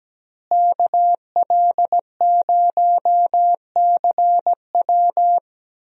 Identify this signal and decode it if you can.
Morse code, heard as a keyed tone, about 17 wpm, 700 Hz. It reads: KL0CW
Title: USA Call Signs - 5 Characters - Repeated 3x - 17wpm